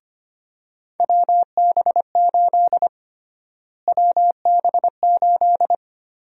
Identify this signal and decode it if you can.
Morse code, heard as a keyed tone, about 25 wpm, 700 Hz. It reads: WB8 WB8
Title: USA Call Sign Prefixes - Repeated 2x - 25wpm